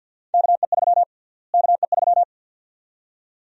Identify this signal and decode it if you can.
Morse code, heard as a keyed tone, about 52 wpm, 700 Hz. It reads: KE3 KE3
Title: USA Call Sign Prefixes - Repeated 2x - 50wpm